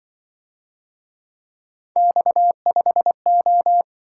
Morse code, heard as a keyed tone, about 24 wpm, 700 Hz. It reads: X5O
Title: Speed-Racing - Three Letter-Number Combinations - 20wpm